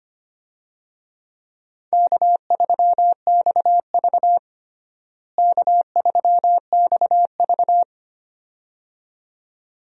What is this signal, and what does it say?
Morse code, heard as a keyed tone, about 25 wpm, 700 Hz. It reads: K3XV K3XV
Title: USA Call Signs - 4 Characters - Repeated 2x - 25wpm